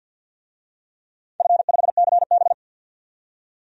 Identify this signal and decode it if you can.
Morse code, heard as a keyed tone, about 53 wpm, 700 Hz. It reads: F5CB
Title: DXCC-100 Call Signs - 4 Characters - 50wpm